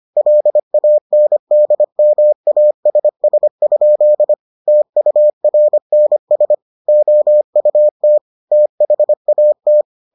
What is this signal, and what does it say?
Morse code, heard as a keyed tone, about 25 wpm, 600 Hz. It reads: LANDMASS? TURNS OUT THAT